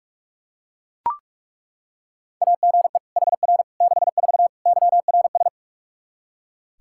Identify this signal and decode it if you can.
Morse code, heard as a keyed tone, about 41 wpm, 700 Hz. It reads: AGE HR 64 YRS
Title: Practice QSO - Exchange 3 - More Info - 45wpm